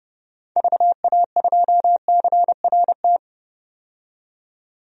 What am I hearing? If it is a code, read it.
Morse code, heard as a keyed tone, about 30 wpm, 700 Hz. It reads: VA2CRT
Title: DXCC-100 Call Signs - 6 Characters - 30wpm